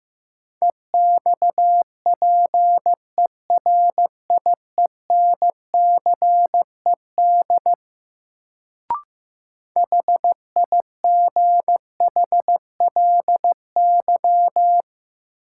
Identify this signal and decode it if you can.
Morse code, heard as a keyed tone, about 15 wpm, 700 Hz. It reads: EXPERIENCED HIGHLY
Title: Sentences from Top 1100 Words - New Word Drill - 15wpm